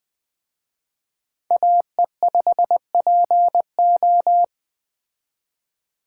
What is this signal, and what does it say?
Morse code, heard as a keyed tone, about 20 wpm, 700 Hz. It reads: AE5PO